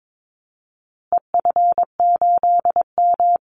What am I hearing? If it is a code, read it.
Morse code, heard as a keyed tone, about 22 wpm, 700 Hz. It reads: EF8M
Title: Call Signs Repeated 3x - 22wpm